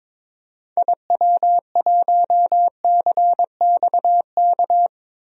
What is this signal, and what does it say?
Morse code, heard as a keyed tone, about 21 wpm, 700 Hz. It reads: IW1CXK
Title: DXCC-100 Call Signs - 6 Characters - Repeated 2x - 22wpm